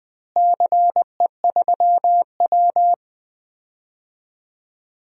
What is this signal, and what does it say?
Morse code, heard as a keyed tone, about 20 wpm, 700 Hz. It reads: CE3W
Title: DXCC-100 Call Signs - 4 Characters - Repeated 3x - 20wpm